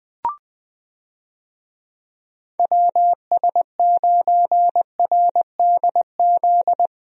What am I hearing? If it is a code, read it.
Morse code, heard as a keyed tone, about 20 wpm, 700 Hz. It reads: WS9RDZ
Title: USA Call Signs - 6 Characters - Repeated 2x - 20wpm